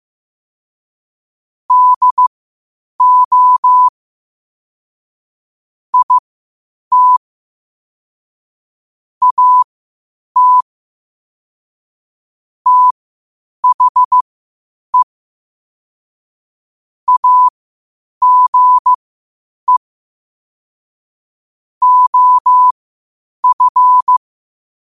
Morse code, heard as a keyed tone, about 15 wpm, spaced as for 8 wpm, 1000 Hz. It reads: DO IT AT THE AGE OF
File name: qotdfiles / 8wpm